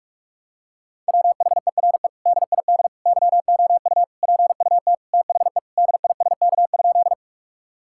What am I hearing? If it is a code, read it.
Morse code, heard as a keyed tone, about 46 wpm, 700 Hz. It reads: WHERE DID YOU PUT THE DISK?
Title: Sentences from Top 1000 Words - Review Part 6 - 45wpm